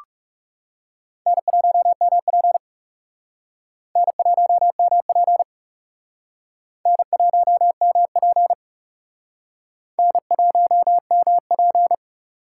Morse code, heard as a keyed tone, about 38 wpm, 700 Hz. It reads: N1MP N1MP N1MP N1MP